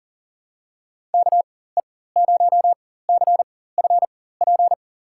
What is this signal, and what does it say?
Morse code, heard as a keyed tone, about 40 wpm, spaced as for 19 wpm, 700 Hz. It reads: KE0CFP
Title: USA Call Signs - 6 Characters - 4x Character Spacing - 40wpm